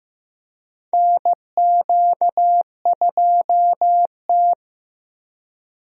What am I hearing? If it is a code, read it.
Morse code, heard as a keyed tone, about 15 wpm, 700 Hz. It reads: NQ2T
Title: USA Call Signs - 4 Characters - Repeated 3x - 15wpm